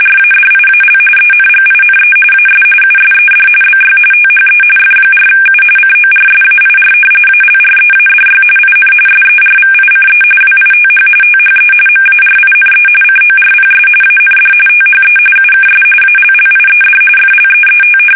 RTTY, about 45.45 baud. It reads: RADIAL ENGINE AS USED BY THE F4U CORSAIR ALREADY IN SERVICE WITH THE MARINE CORPS AND THE UKS ALLIED FLEET
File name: fsk850